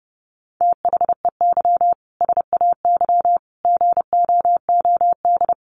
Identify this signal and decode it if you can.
Morse code, heard as a keyed tone, about 30 wpm, 700 Hz. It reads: THEY SAY GOOD
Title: Easy Phrases 30wpm